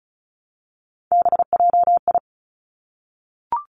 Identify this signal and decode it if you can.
Morse code, heard as a keyed tone, about 36 wpm, 700 Hz. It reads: BJI
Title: Three Challenging Letter-Number Combinations 35wpm